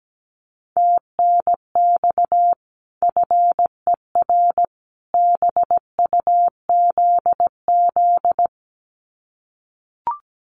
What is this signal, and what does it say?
Morse code, heard as a keyed tone, about 17 wpm, 700 Hz. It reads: TNX FER BUZZ
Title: Common QSO Phrases 17wpm